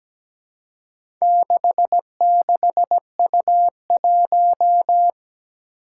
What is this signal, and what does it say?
Morse code, heard as a keyed tone, about 17 wpm, 700 Hz. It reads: 66U1
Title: Four Challenging Letter-Number Combinations 17wpm